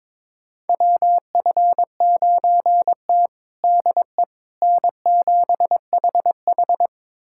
Morse code, heard as a keyed tone, about 22 wpm, 700 Hz. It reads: WF9T DE N7HH
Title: USA Callsign DE Callsign - 4 Characters - 22wpm